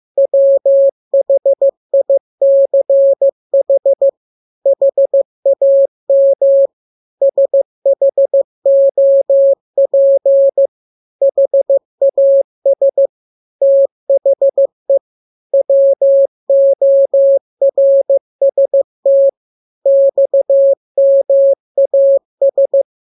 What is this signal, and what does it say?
Morse code, heard as a keyed tone, about 15 wpm, 550 Hz. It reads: WHICH HAM SHOP HAS THE WORST XMAS